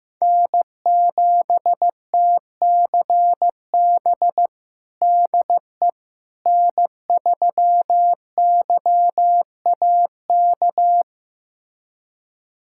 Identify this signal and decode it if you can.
Morse code, heard as a keyed tone, about 15 wpm, 700 Hz. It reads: N7TCB DE N3YAK
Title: USA Callsign DE Callsign - 5 Characters - 15wpm